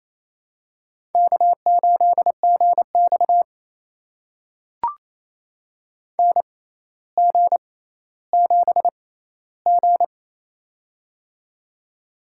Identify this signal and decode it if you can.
Morse code, heard as a keyed tone, about 28 wpm, spaced as for 14 wpm, 700 Hz. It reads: K8GX NG7G
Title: USA Call Signs - 4 Characters - 6x Character Spacing - 28wpm